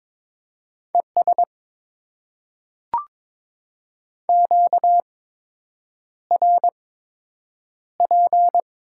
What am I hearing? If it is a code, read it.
Morse code, heard as a keyed tone, about 21 wpm, spaced as for 8 wpm, 700 Hz. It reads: ES QRP
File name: Top 90 QSO Elements - 8x Character Spacing - 22wpm